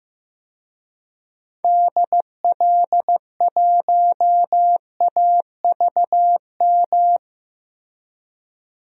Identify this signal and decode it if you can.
Morse code, heard as a keyed tone, about 15 wpm, 700 Hz. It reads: DL1AVM
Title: DXCC-100 Call Signs - 6 Characters - 15wpm